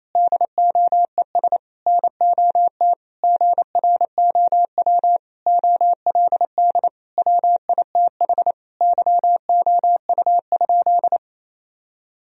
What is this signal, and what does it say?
Morse code, heard as a keyed tone, about 28 wpm, 700 Hz. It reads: DOES NOT GROW OLD WITH YOU?
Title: Sentences from Top 1000 Words - Review Part 1 - 28wpm